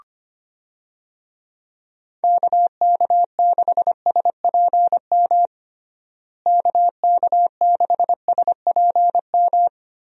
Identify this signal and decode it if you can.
Morse code, heard as a keyed tone, about 25 wpm, 700 Hz. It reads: KK6SPM KK6SPM
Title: USA Call Signs - 6 Characters - Repeated 2x - 25wpm